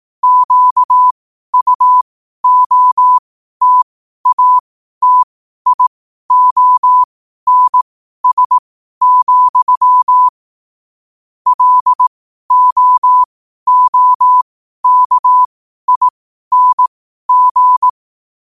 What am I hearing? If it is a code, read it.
Morse code, heard as a keyed tone, about 18 wpm, spaced as for 12 wpm, 1000 Hz. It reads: QUOTATIONS, LOOKING
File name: qotdfiles / 12wpm